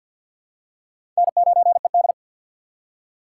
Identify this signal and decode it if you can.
Morse code, heard as a keyed tone, about 49 wpm, 700 Hz. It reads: N9ED
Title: USA Call Signs - 4 Characters - 3x Character Spacing - 50wpm